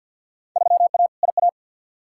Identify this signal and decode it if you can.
Morse code, heard as a keyed tone, about 47 wpm, 700 Hz. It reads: 3A IA
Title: Field Day Exchanges Encore - 50wpm